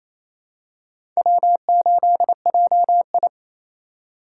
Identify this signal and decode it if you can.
Morse code, heard as a keyed tone, about 28 wpm, 700 Hz. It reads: W8JI
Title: USA Call Signs - 4 Characters - 4x Character Spacing - 28wpm